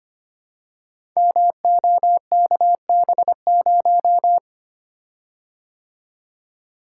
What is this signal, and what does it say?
Morse code, heard as a keyed tone, about 25 wpm, 700 Hz. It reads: MOKB0
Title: Code Groups - 5 Letter:Numbers - 25wpm